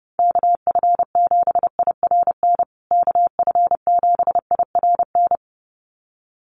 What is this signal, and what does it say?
Morse code, heard as a keyed tone, about 29 wpm, 700 Hz. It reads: KF7IRN KF7IRN
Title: USA Call Signs Repeated 2x 30wpm